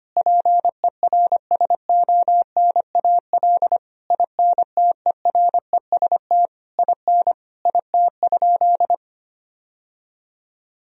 Morse code, heard as a keyed tone, about 25 wpm, 700 Hz. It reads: PERSONAL INTEREST IN IT?
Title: Sentences from Top 1000 Words - Review Part 4 - 25wpm